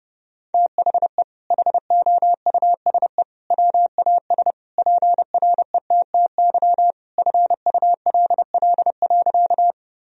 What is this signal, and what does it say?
Morse code, heard as a keyed tone, about 30 wpm, 700 Hz. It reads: THE HOUSE WAS PRETTY FULL.
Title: Sentences from Top 1000 Words - Review Part 3 - 30wpm